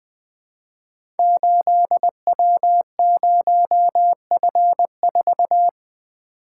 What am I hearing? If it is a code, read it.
Morse code, heard as a keyed tone, about 20 wpm, 700 Hz. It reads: 8W0F4
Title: Code Groups - 5 Letter:Numbers - 20wpm